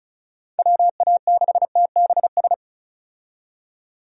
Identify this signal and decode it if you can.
Morse code, heard as a keyed tone, about 36 wpm, 700 Hz. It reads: WA6TBS
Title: USA Call Signs - 6 Characters - 35wpm